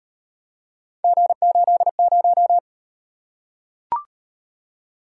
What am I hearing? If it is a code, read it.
Morse code, heard as a keyed tone, about 39 wpm, 700 Hz. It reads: G80